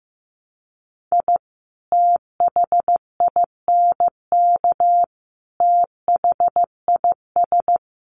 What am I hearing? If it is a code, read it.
Morse code, heard as a keyed tone, about 15 wpm, 700 Hz. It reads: I THINK THIS
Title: Sets of 5 Words 15wpm